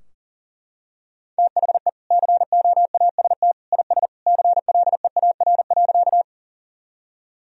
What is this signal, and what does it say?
Morse code, heard as a keyed tone, about 40 wpm, 700 Hz. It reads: THE COAST IS CLEAR.
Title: Sentences from Top 1000 Words - Review Part 4 - 40wpm